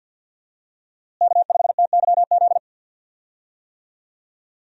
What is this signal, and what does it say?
Morse code, heard as a keyed tone, about 48 wpm, 700 Hz. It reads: K5TYZ
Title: USA Call Signs - 5 Characters - Repeated 3x - 50wpm